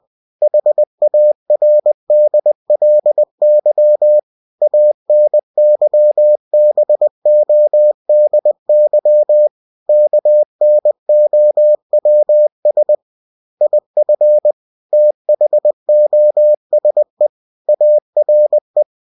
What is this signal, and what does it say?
Morse code, heard as a keyed tone, about 20 wpm, 600 Hz. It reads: HARDLY ANYBODY KNOWS IF THOSE ARE
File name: cw-books / winnie-the-pooh / winnie-20wpm-page-0068